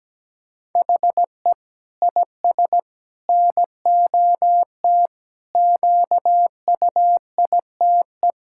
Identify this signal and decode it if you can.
Morse code, heard as a keyed tone, about 17 wpm, 700 Hz. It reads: HE IS NOT QUITE
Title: Sentences from Top 1600 Words 17wpm